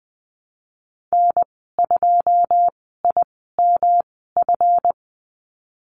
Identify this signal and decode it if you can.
Morse code, heard as a keyed tone, about 20 wpm, spaced as for 15 wpm, 700 Hz. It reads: N2IMF
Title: 179 Lesson 29g - Slash - Call Signs 20wpm